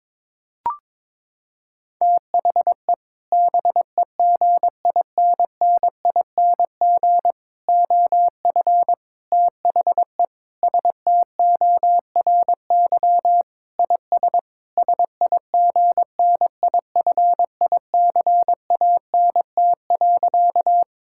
Morse code, heard as a keyed tone, about 22 wpm, 700 Hz. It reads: THE BEGINNING OF THE STORY IS SIGNIFICANT.
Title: Sentences from Top 1000 Words - Review Part 6 - 22wpm